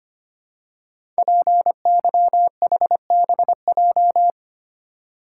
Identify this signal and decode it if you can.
Morse code, heard as a keyed tone, about 25 wpm, 700 Hz. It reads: PYHBJ
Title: Code Groups - 5 Letters - 25wpm